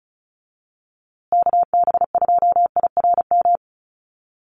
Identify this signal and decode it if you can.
Morse code, heard as a keyed tone, about 34 wpm, 700 Hz. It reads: KB2IRM
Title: USA Call Signs Repeated 2x 35wpm